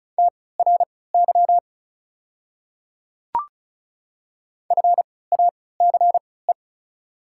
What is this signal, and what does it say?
Morse code, heard as a keyed tone, about 36 wpm, spaced as for 20 wpm, 690 Hz. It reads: TRY FACE
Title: 3x Character Spacing - Top 300 Words - 35wpm